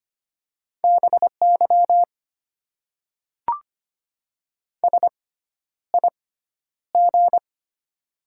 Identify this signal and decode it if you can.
Morse code, heard as a keyed tone, about 25 wpm, spaced as for 11 wpm, 700 Hz. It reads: BY SIG
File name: Top 90 QSO Elements - 6x Character Spacing - 25wpm